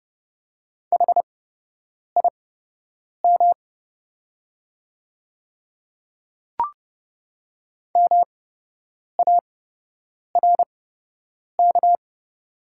Morse code, heard as a keyed tone, about 30 wpm, spaced as for 8 wpm, 700 Hz. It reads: HIM MARK
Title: 8x Character Spacing - Top 500 Words - 30wpm